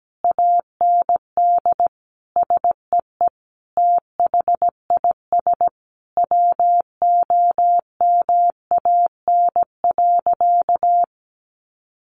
Morse code, heard as a keyed tone, about 17 wpm, 700 Hz. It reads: AND SEE THIS WOMAN.
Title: Sentences from Top 1100 Words 17wpm